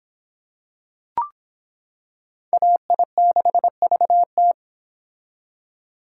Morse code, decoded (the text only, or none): AI6VT